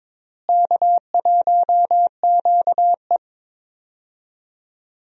Morse code, heard as a keyed tone, about 22 wpm, 700 Hz. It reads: K1QE
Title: USA Call Signs - 4 Characters - Repeated 2x - 22wpm